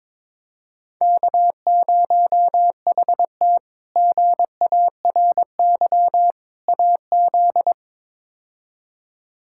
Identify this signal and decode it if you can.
Morse code, heard as a keyed tone, about 22 wpm, 700 Hz. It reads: K0HT GARY AZ
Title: K1USN SST QSO Practice - 22wpm